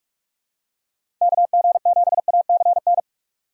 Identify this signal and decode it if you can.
Morse code, heard as a keyed tone, about 43 wpm, 700 Hz. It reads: KG7ACN
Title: USA Call Signs - 6 Characters - 3x Character Spacing - 45wpm